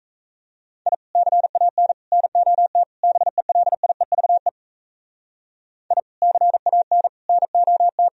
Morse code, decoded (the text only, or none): I CAN NOT BELIEVE I CAN NOT